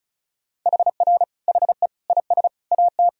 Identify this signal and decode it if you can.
Morse code, heard as a keyed tone, about 34 wpm, 700 Hz. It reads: HR HE IS AT
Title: Sentences from Top 300 Words and Abbreviations 35wpm